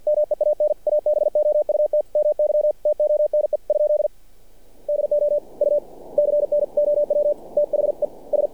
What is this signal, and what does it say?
Morse code, heard as a keyed tone, about 47 wpm, 600 Hz. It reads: MEAN ABOUT MY TONE ? DO U CNOW THE S
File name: archived / CRSnet